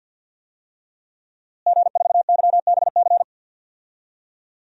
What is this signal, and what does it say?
Morse code, heard as a keyed tone, about 52 wpm, 700 Hz. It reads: G4YBC
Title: DXCC-100 Call Signs - 5 Characters - Repeated 2x - 50wpm